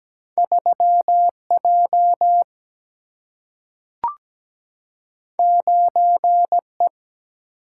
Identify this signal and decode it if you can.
Morse code, heard as a keyed tone, about 17 wpm, 700 Hz. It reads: 3 J 9 E
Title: Single-Letter Single-Number 17wpm